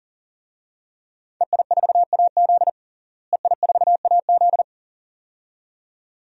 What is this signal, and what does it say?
Morse code, heard as a keyed tone, about 40 wpm, 700 Hz. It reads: EI4AZ EI4AZ
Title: DXCC-100 Call Signs - 5 Characters - Repeated 2x - 40wpm